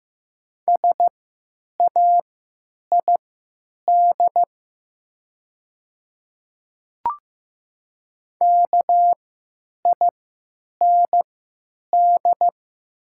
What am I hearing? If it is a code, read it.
Morse code, heard as a keyed tone, about 15 wpm, spaced as for 9 wpm, 700 Hz. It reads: SAID KIND
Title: 3x Character Spacing - Top 300 Words - 15wpm